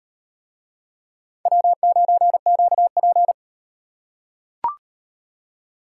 Morse code, heard as a keyed tone, about 38 wpm, 700 Hz. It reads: W9QP